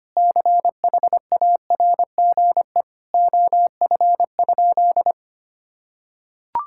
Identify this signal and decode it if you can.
Morse code, heard as a keyed tone, about 25 wpm, 700 Hz. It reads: CHARGE OF?
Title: Sentences from Top 1000 Words - Review Part 6 - 25wpm